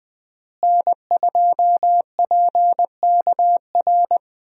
Morse code, heard as a keyed tone, about 20 wpm, 700 Hz. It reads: N2PKR
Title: USA Call Signs - 5 Characters - Repeated 3x - 20wpm